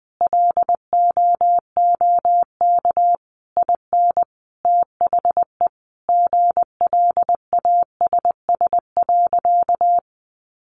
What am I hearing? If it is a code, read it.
Morse code, heard as a keyed tone, about 19 wpm, 700 Hz. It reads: LOOK IN THE GLASS.
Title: Sentences from Top 1300 Words 20wpm